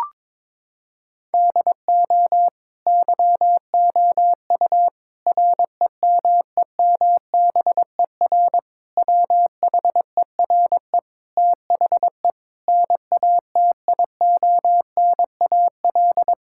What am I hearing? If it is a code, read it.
Morse code, heard as a keyed tone, about 22 wpm, 700 Hz. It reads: DO YOU REMEMBER WHERE THE NATIONAL
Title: Sentences from Top 1000 Words - Review Part 6 - 22wpm